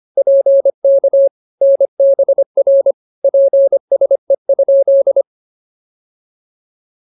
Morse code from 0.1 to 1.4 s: PK

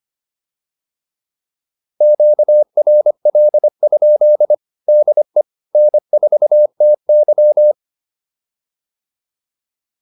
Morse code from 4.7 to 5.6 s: DE

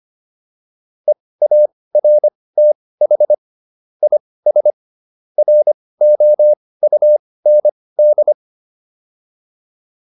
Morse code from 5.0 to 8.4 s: ROUND

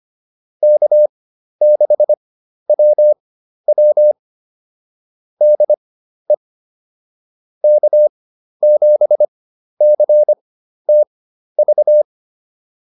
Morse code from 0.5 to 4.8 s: K6WW

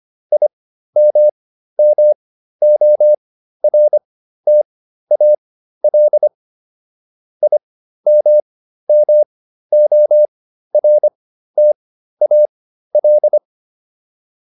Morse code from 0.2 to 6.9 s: IMMORTAL

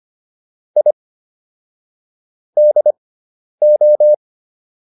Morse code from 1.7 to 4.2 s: DO

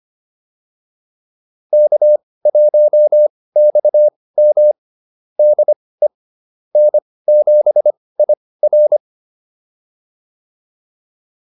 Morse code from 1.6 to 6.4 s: K1XM DE

